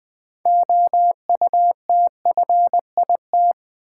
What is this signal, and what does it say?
Morse code, heard as a keyed tone, about 20 wpm, 700 Hz. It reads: OUTFIT
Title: Build A Word - 3 Letters At A Time - 6-Letter Words - 20wpm